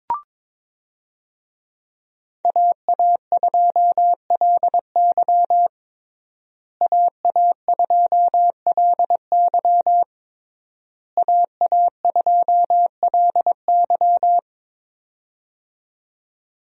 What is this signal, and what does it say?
Morse code, heard as a keyed tone, about 22 wpm, 700 Hz. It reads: AA2LY AA2LY AA2LY
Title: USA Call Signs - 5 Characters - Repeated 3x - 22wpm